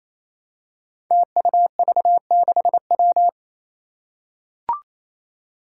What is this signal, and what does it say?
Morse code, heard as a keyed tone, about 28 wpm, 700 Hz. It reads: TUV6W
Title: Five Challenging Letter-Number Combinations 28wpm